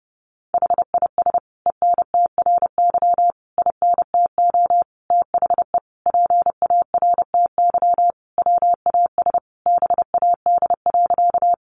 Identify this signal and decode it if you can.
Morse code, heard as a keyed tone, about 30 wpm, 700 Hz. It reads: HIS ENTRY INTO THE PARTY WAS BAD.
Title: Sentences from Top 1500 Words 30wpm